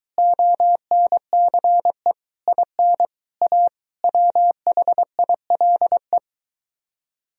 Morse code, heard as a keyed tone, about 23 wpm, 700 Hz. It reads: ONCE IN A WHILE